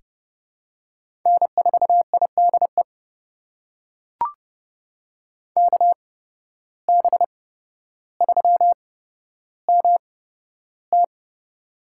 Morse code, heard as a keyed tone, about 29 wpm, spaced as for 13 wpm, 700 Hz. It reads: N4IDE KB3MT
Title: USA Call Signs - 5 Characters - 8x Character Spacing - 30wpm